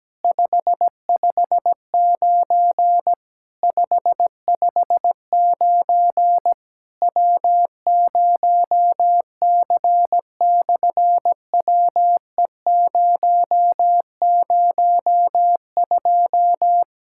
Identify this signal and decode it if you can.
Morse code, heard as a keyed tone, about 17 wpm, 700 Hz. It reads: 559 559 W0C/WE002
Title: SOTA QSO Practice 17wpm